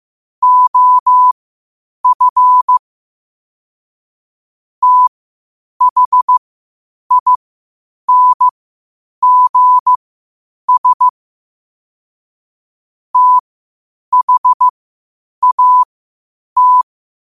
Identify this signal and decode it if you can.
Morse code, heard as a keyed tone, about 15 wpm, spaced as for 8 wpm, 1000 Hz. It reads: OF THINGS THAT
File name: qotdfiles / 8wpm